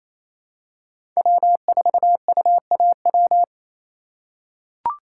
Morse code, heard as a keyed tone, about 28 wpm, 700 Hz. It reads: W4UAW
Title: USA Call Signs - 5 Characters - 4x Character Spacing - 28wpm